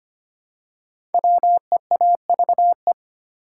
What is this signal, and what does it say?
Morse code, heard as a keyed tone, about 25 wpm, 700 Hz. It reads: WEAVE
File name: 5 Letter Words - First 3 Letters Given - 25wpm